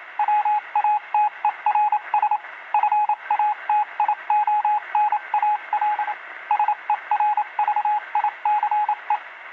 Morse code, heard as a keyed tone, about 27 wpm, 840 Hz, with radio noise. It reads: WATERS FATIONAL SERVICE